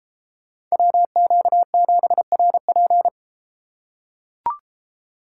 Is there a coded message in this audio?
Yes: Morse code from 0.7 to 3.1 s, heard as a keyed tone, about 31 wpm, 700 Hz. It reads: WQ7RP